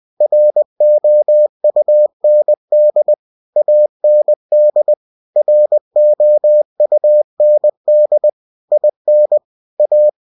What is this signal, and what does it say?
Morse code, heard as a keyed tone, about 20 wpm, 600 Hz. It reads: ROUND AND ROUND IN A
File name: cw-books / winnie-the-pooh / winnie-20wpm-page-0021